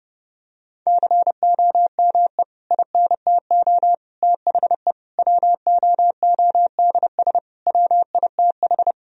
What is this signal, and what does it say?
Morse code, heard as a keyed tone, about 30 wpm, 700 Hz. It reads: COME INTO THE WOODS WITH
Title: Sentences from Top 1000 Words - Review Part 2 - 30wpm